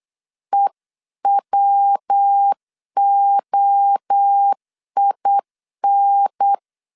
Morse code, heard as a keyed tone, about 8 wpm, 800 Hz. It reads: EWOIN